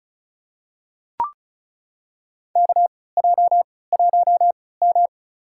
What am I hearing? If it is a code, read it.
Morse code, heard as a keyed tone, about 34 wpm, spaced as for 20 wpm, 700 Hz. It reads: KJ1M
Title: USA Call Signs - 4 Characters - 3x Character Spacing - 35wpm